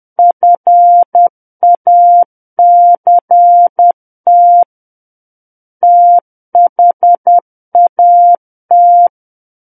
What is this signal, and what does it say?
Morse code, heard as a keyed tone, about 10 wpm, 700 Hz. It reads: FACT THAT
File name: Book One / 01-Chapter 1 - 10 WPM